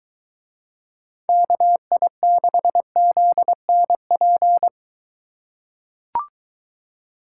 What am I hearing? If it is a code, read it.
Morse code, heard as a keyed tone, about 22 wpm, 700 Hz. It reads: KI6ZNP